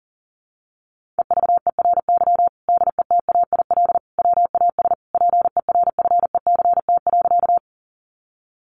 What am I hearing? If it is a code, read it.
Morse code, heard as a keyed tone, about 40 wpm, 700 Hz. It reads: EVERY DETAIL WAS PERFECT.
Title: Sentences from Top 1500 Words 40wpm